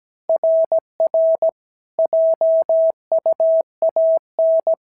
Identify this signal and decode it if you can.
Morse code, heard as a keyed tone, about 17 wpm, 650 Hz. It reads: RR JUAN
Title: SOTA QSO Practice 17wpm